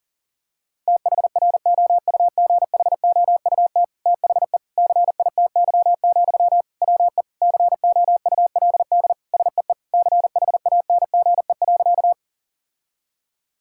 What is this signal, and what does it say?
Morse code, heard as a keyed tone, about 40 wpm, 700 Hz. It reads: THROUGHOUT THE CITY, WE COULD SEE CHANGE.
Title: Sentences from Top 1000 Words - Review Part 5 - 40wpm